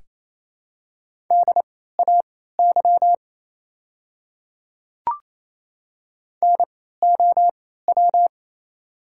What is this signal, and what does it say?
Morse code, heard as a keyed tone, about 28 wpm, spaced as for 16 wpm, 700 Hz. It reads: DAY NOW
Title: 3x Character Spacing - Top 200 Words - 28wpm